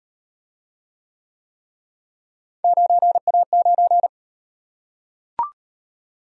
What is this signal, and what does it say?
Morse code, heard as a keyed tone, about 38 wpm, 700 Hz. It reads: 9A9